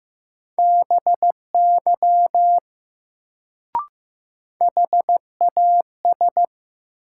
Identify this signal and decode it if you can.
Morse code, heard as a keyed tone, about 15 wpm, 700 Hz. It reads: BY HAS
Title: Sets of 2 Words - New Word Drill - 15wpm